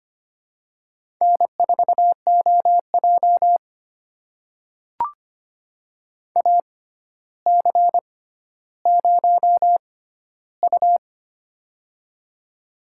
Morse code, heard as a keyed tone, about 25 wpm, spaced as for 13 wpm, 700 Hz. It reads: N4OJ AC0U
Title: USA Call Signs - 4 Characters - 6x Character Spacing - 25wpm